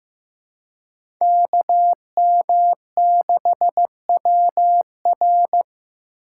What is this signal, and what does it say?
Morse code, heard as a keyed tone, about 15 wpm, 700 Hz. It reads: KM6WR
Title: USA Call Signs - 3x Character Spacing - 15wpm